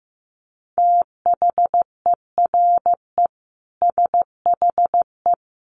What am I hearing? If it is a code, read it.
Morse code, heard as a keyed tone, about 14 wpm, 700 Hz. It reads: THERE SHE
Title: Sentences from Top 200 Words 15wpm